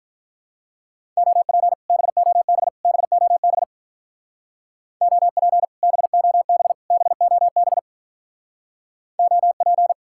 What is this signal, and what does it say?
Morse code, heard as a keyed tone, about 45 wpm, 700 Hz. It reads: OP BOB BOB OP BOB BOB OP